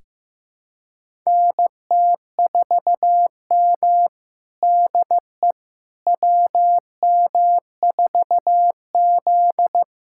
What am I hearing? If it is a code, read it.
Morse code, heard as a keyed tone, about 15 wpm, 700 Hz. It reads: NT4M DE WM4Z
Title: USA Callsign DE Callsign - 4 Characters - 15wpm